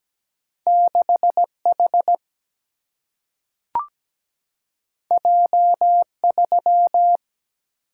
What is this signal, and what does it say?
Morse code, heard as a keyed tone, about 17 wpm, 700 Hz. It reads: 6 H J 3
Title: Single-Letter Single-Number 17wpm